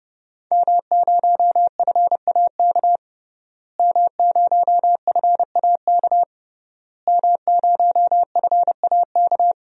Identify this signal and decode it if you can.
Morse code, heard as a keyed tone, about 30 wpm, 700 Hz. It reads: M0FAK M0FAK M0FAK
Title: DXCC-100 Call Signs - 5 Characters - Repeated 3x - 30wpm